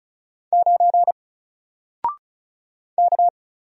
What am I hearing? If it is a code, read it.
Morse code, heard as a keyed tone, about 34 wpm, 700 Hz. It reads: 9 K
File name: Single-Letter-Number-35wpm